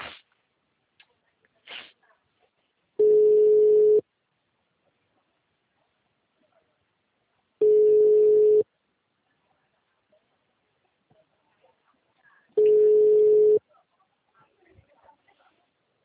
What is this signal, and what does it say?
Ringback tone (Europe, 425 Hz).